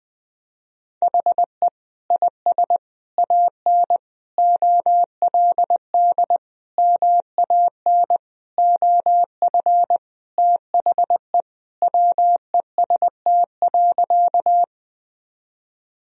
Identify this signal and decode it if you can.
Morse code, heard as a keyed tone, about 20 wpm, 700 Hz. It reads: HE IS AN OLD MAN OF THE WEST.
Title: Sentences from Top 1000 Words - Review Part 2 - 20wpm